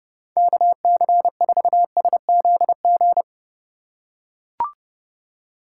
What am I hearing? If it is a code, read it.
Morse code, heard as a keyed tone, about 30 wpm, 700 Hz. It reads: KC4SZG
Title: USA Call Signs - 6 Characters - 2x Character Spacing - 30wpm